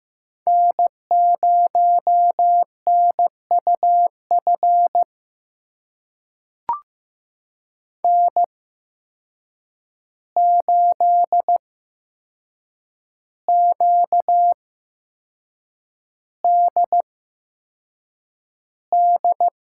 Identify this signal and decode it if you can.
Morse code, heard as a keyed tone, about 15 wpm, spaced as for 6 wpm, 700 Hz. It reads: N0NUF N8QDD
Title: USA Call Signs - 5 Characters - 8x Character Spacing - 15wpm